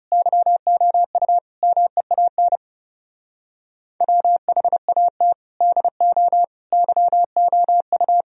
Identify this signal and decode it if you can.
Morse code, heard as a keyed tone, about 37 wpm, 700 Hz. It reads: YOU MEAN WHAT DO YOU
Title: Speed-Racing - Sets of 4 Words 30wpm